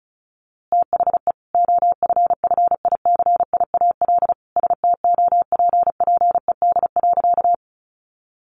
Morse code, heard as a keyed tone, about 34 wpm, 700 Hz. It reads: THE OFFICIAL STOPPED.
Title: A Sound Of Thunder 35wpm